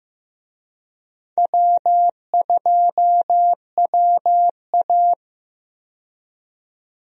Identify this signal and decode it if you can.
Morse code, heard as a keyed tone, about 15 wpm, 700 Hz. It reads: W2WA
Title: USA Call Signs - 4 Characters - 15wpm